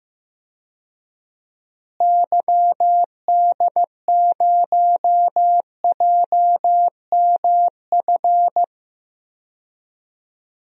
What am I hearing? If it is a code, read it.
Morse code, heard as a keyed tone, about 15 wpm, 700 Hz. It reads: YD0JMF
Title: DXCC-100 Call Signs - 6 Characters - Repeated 3x - 15wpm